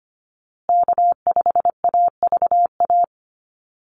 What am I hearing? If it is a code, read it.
Morse code, heard as a keyed tone, about 25 wpm, 700 Hz. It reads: K5AVA
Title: USA Call Signs Repeated 3x 25wpm